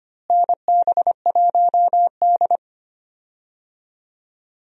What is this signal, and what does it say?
Morse code, heard as a keyed tone, about 25 wpm, 700 Hz. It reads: NB1D
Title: USA Call Signs - 4 Characters - Repeated 3x - 25wpm